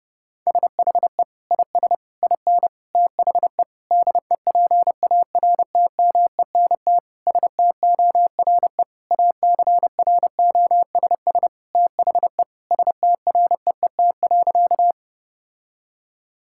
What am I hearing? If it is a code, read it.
Morse code, heard as a keyed tone, about 32 wpm, 700 Hz. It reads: SHE IS IN THE DEPARTMENT STORE ACROSS THE STREET.
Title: Sentences from Top 1000 Words - Review Part 5 - 30wpm